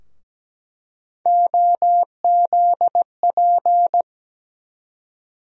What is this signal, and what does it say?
Morse code, heard as a keyed tone, about 17 wpm, 700 Hz. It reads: OZP
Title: USA Call Sign Postfixes - 3 Letters 17wpm